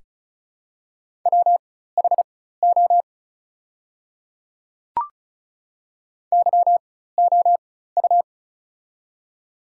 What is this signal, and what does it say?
Morse code, heard as a keyed tone, about 34 wpm, spaced as for 16 wpm, 690 Hz. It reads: WHO YOU
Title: 4x Character Spacing - Top 100 Words - 35wpm